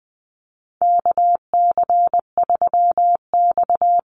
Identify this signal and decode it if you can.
Morse code, heard as a keyed tone, about 20 wpm, 700 Hz. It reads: KC3X
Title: Call Signs Repeated 2x - 20wpm